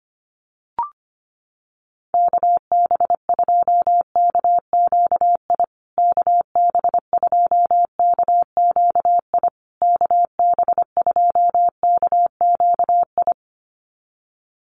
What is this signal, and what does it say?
Morse code, heard as a keyed tone, about 25 wpm, 700 Hz. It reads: KB2KQI KB2KQI KB2KQI
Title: USA Call Signs Repeated 3x 25wpm